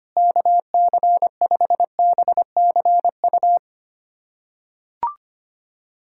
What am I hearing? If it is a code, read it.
Morse code, heard as a keyed tone, about 25 wpm, 700 Hz. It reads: KC5BCU
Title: USA Callsign DE Callsign - 6 Characters - 25wpm